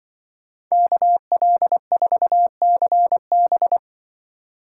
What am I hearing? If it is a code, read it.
Morse code, heard as a keyed tone, about 25 wpm, 700 Hz. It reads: KL4CB
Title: Speed Racing - USA Call Signs 20wpm